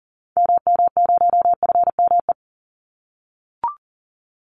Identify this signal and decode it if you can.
Morse code, heard as a keyed tone, about 38 wpm, 700 Hz. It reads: MM0FME
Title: Call Signs - 40wpm